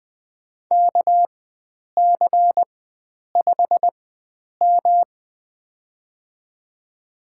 Morse code, heard as a keyed tone, about 21 wpm, spaced as for 9 wpm, 700 Hz. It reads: KC5M
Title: USA Call Signs - 4 Characters - 4x Character Spacing - 20wpm